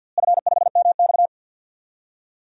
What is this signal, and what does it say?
Morse code, heard as a keyed tone, about 52 wpm, 700 Hz. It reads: F5MX
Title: DXCC-100 Call Signs - 4 Characters - Repeated 3x - 50wpm